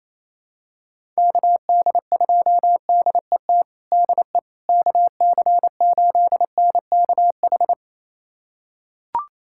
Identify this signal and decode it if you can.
Morse code, heard as a keyed tone, about 28 wpm, 700 Hz. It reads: KD2DET DE KC8NKH
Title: USA Callsign DE Callsign - 6 Characters - 28wpm